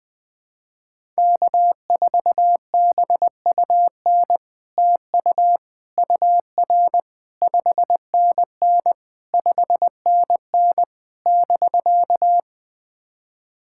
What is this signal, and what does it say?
Morse code, heard as a keyed tone, about 20 wpm, 690 Hz. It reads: K4BUN TU UR 5NN 5NN <BK>
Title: SOTA QSO Practice 20wpm